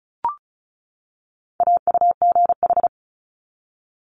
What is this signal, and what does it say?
Morse code, heard as a keyed tone, about 36 wpm, 700 Hz. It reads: AUGH
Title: Multi-letter Phonemes 35wpm